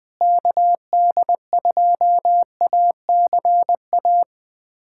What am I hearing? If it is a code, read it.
Morse code, heard as a keyed tone, about 20 wpm, 700 Hz. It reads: KD2ACA
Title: USA Callsign DE Callsign 20wpm